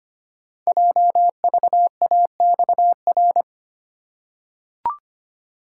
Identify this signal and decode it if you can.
Morse code, heard as a keyed tone, about 25 wpm, 690 Hz. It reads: JVAXR
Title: Code Groups - 5 Letter:Numbers - 25wpm